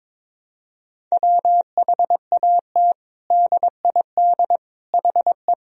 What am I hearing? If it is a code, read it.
Morse code, heard as a keyed tone, about 22 wpm, 700 Hz. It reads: WHAT DID HE
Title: Sentences from Top 100 Words and Abbreviations 22wpm